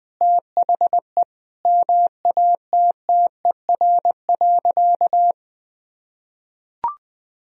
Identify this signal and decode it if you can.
Morse code, heard as a keyed tone, about 20 wpm, 700 Hz. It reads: THE MATTER.
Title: Sentences from Top 1000 Words - Review Part 4 - 20wpm